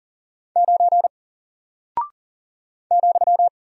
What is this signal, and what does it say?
Morse code, heard as a keyed tone, about 40 wpm, 700 Hz. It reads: 9 ,
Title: Single Character (Letter:Number:Punctuation:Prosign) Alt 40wpm